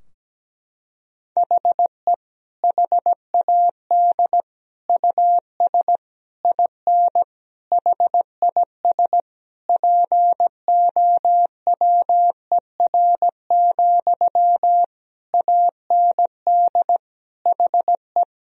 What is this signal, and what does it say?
Morse code, heard as a keyed tone, about 17 wpm, 700 Hz. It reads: HE HAD US IN HIS POWER, AND HE
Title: Sentences from Top 1000 Words - Review Part 2 - 17wpm